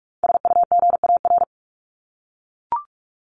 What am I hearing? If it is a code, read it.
Morse code, heard as a keyed tone, about 44 wpm, 700 Hz. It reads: SUGAR
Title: Top 1001-1250 Words 45wpm